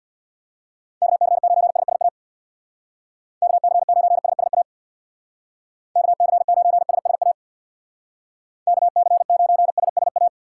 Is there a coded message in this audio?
Yes: Morse code from 1.0 to 10.3 s, heard as a keyed tone, about 65 wpm, 700 Hz. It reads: KC9SSA KC9SSA KC9SSA KC9SSA